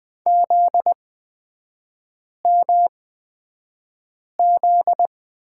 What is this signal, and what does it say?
Morse code, heard as a keyed tone, about 20 wpm, spaced as for 5 wpm, 700 Hz. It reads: ZMZ
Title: 243 Lesson 39e - Letter z - Three Characters 20wpm